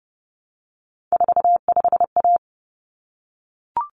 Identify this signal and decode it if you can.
Morse code, heard as a keyed tone, about 30 wpm, 700 Hz. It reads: 45A
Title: Common Cut Numbers - 3-Digit 30wpm